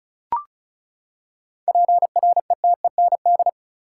Morse code, heard as a keyed tone, about 34 wpm, 700 Hz. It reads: PRETEND
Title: Seven Letter Words - 35wpm